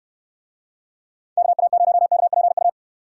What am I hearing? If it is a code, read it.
Morse code, heard as a keyed tone, about 63 wpm, 700 Hz. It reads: KA0KPU